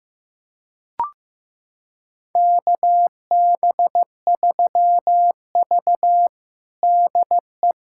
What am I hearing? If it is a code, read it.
Morse code, heard as a keyed tone, about 15 wpm, 700 Hz. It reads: KB3V DE
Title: USA Callsign DE Callsign - 4 Characters - 15wpm